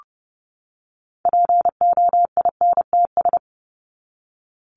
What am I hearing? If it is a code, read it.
Morse code, heard as a keyed tone, about 30 wpm, 700 Hz. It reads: POINTS
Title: ES:S Suffix - Top 100 Words with Base Words 30wpm